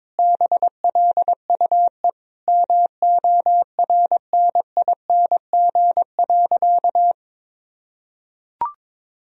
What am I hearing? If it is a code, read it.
Morse code, heard as a keyed tone, about 22 wpm, 700 Hz. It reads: BLUE MORNING.
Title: Sentences from Top 1000 Words - Review Part 3 - 22wpm